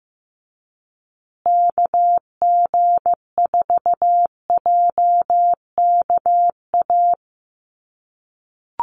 Morse code, heard as a keyed tone, about 15 wpm, 700 Hz. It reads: KG4JKA
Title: USA Call Signs Repeated 2x 15wpm